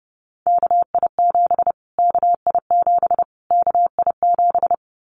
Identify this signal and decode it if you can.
Morse code, heard as a keyed tone, about 28 wpm, 700 Hz. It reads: KI7 KI7 KI7
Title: Callsign Prefixes for Top 25 DXCC - Repeated 3x - 30wpm